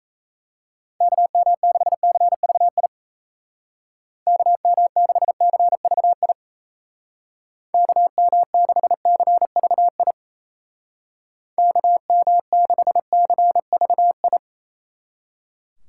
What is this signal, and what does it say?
Morse code, heard as a keyed tone, about 34 wpm, 700 Hz. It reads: KM6CVI KM6CVI KM6CVI KM6CVI
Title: Speed Racing - USA 2x3 Call Signs 28wpm